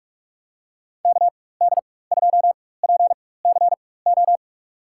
Morse code, heard as a keyed tone, about 46 wpm, spaced as for 21 wpm, 700 Hz. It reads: KD2PCO
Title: USA Call Signs - 6 Characters - 4x Character Spacing - 45wpm